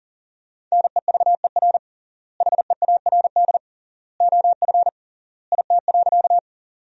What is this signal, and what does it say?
Morse code, heard as a keyed tone, about 40 wpm, 700 Hz. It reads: NEVER HEARD OF IT.
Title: 3x Word-Spacing - Sentences from Top 400 Words - 40wpm